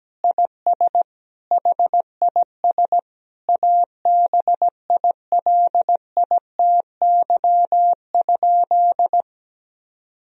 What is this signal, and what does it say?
Morse code, heard as a keyed tone, about 17 wpm, 700 Hz. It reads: IS HIS ABILITY?
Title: Sentences from Top 1000 Words - Review Part 4 - 17wpm